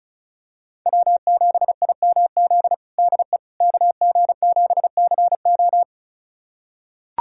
Morse code, heard as a keyed tone, about 34 wpm, 700 Hz. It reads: W7IMZ DE KG7CO
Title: USA Callsign DE Callsign - 5 Characters - 35wpm